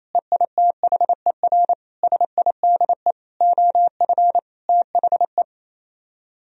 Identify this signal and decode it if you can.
Morse code, heard as a keyed tone, about 36 wpm, 700 Hz. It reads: EITHER SIDE OF THE